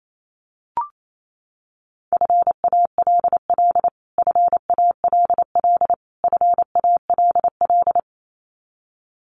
Morse code, heard as a keyed tone, about 28 wpm, 700 Hz. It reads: FALL FALL FALL
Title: Top 500 Words Repeated 28wpm